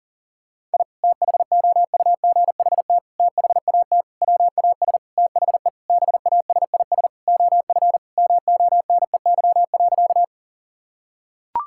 Morse code, heard as a keyed tone, about 37 wpm, 700 Hz. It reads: I THOUGHT THAT WAS THE BASIS OF MONEY.
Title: Sentences from Top 1000 Words - Review Part 5 - 40wpm